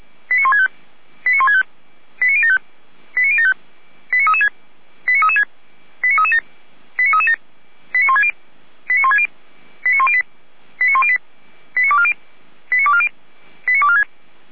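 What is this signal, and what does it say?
Five-tone selective call (ZVEI), heard as quick runs of five tones, tones of about 80 ms. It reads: 89165 89165 89075 89075 89336 89336 89337 89337 89160 89160 89108 89108 89240 89240 89246
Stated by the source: ZVEI five-tone selective call received on an SDR.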